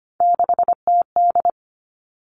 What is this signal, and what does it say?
Morse code, heard as a keyed tone, about 25 wpm, 700 Hz. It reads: 6TD
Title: Three Challenging Letter-Number Combinations 25wpm